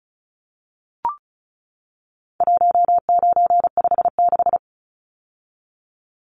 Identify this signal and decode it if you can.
Morse code, heard as a keyed tone, about 34 wpm, 700 Hz. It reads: 1956
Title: Numbers - 4-Digit 35wpm